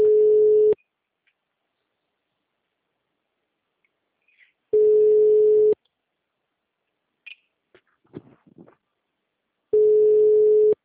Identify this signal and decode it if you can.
Ringback tone (Europe, 425 Hz).
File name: out- -2011- - - .4